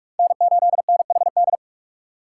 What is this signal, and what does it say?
Morse code, heard as a keyed tone, about 44 wpm, 700 Hz. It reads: N8NHD